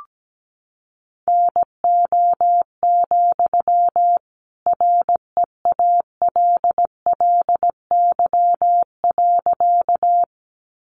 Morse code, heard as a keyed tone, about 17 wpm, 700 Hz. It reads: NO, REALLY.
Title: Sentences from Top 600 Words 17wpm